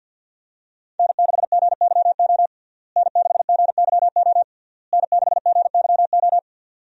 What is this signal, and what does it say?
Morse code, heard as a keyed tone, about 51 wpm, 700 Hz. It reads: N6GYO N6GYO N6GYO
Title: USA Call Signs - 5 Characters - Repeated 3x - 50wpm